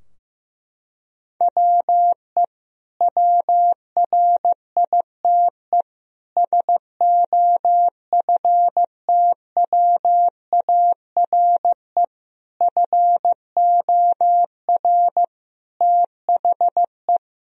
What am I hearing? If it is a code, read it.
Morse code, heard as a keyed tone, about 15 wpm, 700 Hz. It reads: WE WRITE SOFTWARE FOR THE
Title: Sentences from Top 1000 Words - Review Part 3 - 15wpm